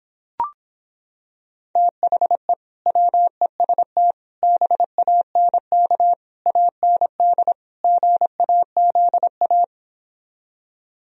Morse code, heard as a keyed tone, about 26 wpm, 700 Hz. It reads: THE WEST BANK AND GAZA